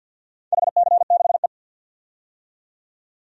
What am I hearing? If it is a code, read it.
Morse code, heard as a keyed tone, about 46 wpm, 700 Hz. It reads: HC6E
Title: DXCC-100 Call Signs - 4 Characters - 50wpm